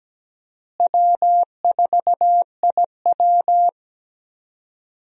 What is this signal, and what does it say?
Morse code, heard as a keyed tone, about 17 wpm, 700 Hz. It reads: W4IW
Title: USA Call Signs - 4 Characters - Repeated 2x - 17wpm